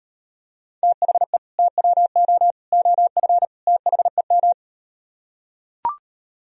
Morse code, heard as a keyed tone, about 38 wpm, 700 Hz. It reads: THE TWO OF THEM